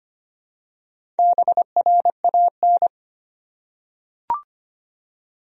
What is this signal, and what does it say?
Morse code, heard as a keyed tone, about 25 wpm, 700 Hz. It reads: BRAN
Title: USA Common Names - 25wpm